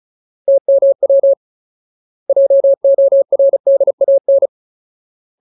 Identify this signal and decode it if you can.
Morse code, heard as a keyed tone, about 34 wpm, 550 Hz. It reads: TMW JORDAN
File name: Ultimate Word List / 35WPM Ultimate